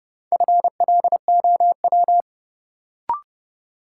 Morse code, heard as a keyed tone, about 31 wpm, 700 Hz. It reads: FLOW
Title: Sentences from Top 1200 Words - New Word Drill Alt - 30wpm